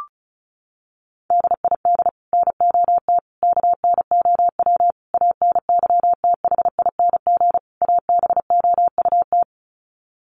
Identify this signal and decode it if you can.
Morse code, heard as a keyed tone, about 34 wpm, 700 Hz. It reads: DID NOT KNOW ANYTHING ABOUT
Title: Sets of 5 Words Encore2 35wpm